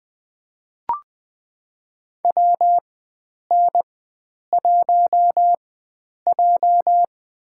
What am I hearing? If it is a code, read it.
Morse code, heard as a keyed tone, about 20 wpm, spaced as for 9 wpm, 700 Hz. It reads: WN1J
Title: USA Call Signs - 4 Characters - 4x Character Spacing - 20wpm